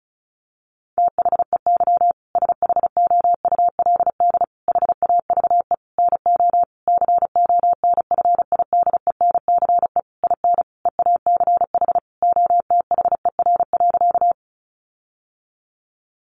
Morse code, heard as a keyed tone, about 34 wpm, 700 Hz. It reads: THEY SHOULD HAVE NO CONFIDENCE IN EACH OTHER.
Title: Sentences from Top 1500 Words 35wpm